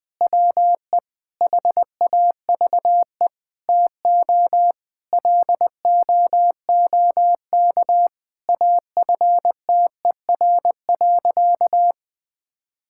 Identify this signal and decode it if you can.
Morse code, heard as a keyed tone, about 21 wpm, 700 Hz. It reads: WE HAVE TO LOOK AFTER.
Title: Sentences from Top 1600 Words 20wpm